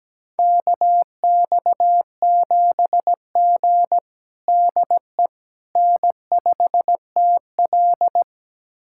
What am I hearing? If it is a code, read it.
Morse code, heard as a keyed tone, about 17 wpm, 700 Hz. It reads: KX7G DE N5TL
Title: USA Callsign DE Callsign - 4 Characters - 17wpm